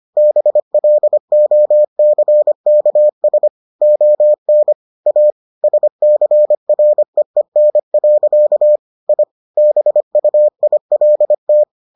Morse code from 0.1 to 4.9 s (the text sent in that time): BLOCKS ON